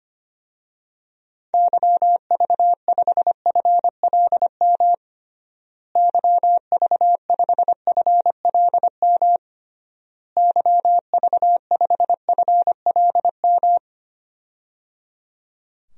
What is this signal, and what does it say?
Morse code, heard as a keyed tone, about 25 wpm, 700 Hz. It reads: YV5FLM YV5FLM YV5FLM
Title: DXCC-100 Call Signs - 6 Characters - Repeated 3x - 25wpm